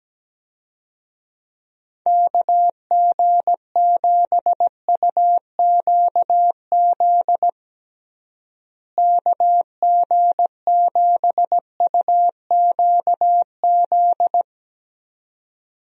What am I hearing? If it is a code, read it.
Morse code, heard as a keyed tone, about 17 wpm, 700 Hz. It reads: KG7UQZ KG7UQZ
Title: USA Call Signs - 6 Characters - Repeated 2x - 17wpm